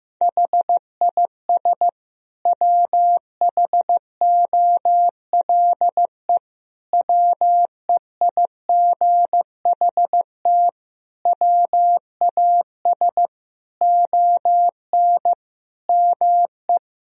Morse code, heard as a keyed tone, about 15 wpm, 700 Hz. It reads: HIS WHOLE WEIGHT WAS ON ME
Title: Sentences from Top 1000 Words - Review Part 3 - 15wpm